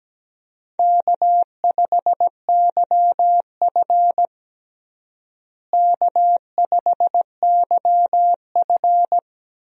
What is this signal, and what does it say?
Morse code, heard as a keyed tone, about 17 wpm, 700 Hz. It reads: K5YF K5YF
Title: USA Call Signs - 4 Characters - Repeated 2x - 17wpm